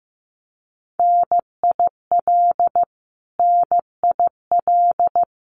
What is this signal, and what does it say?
Morse code, heard as a keyed tone, about 15 wpm, 700 Hz. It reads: NIL NIL
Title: Abbreviations Repeated 3x - 15wpm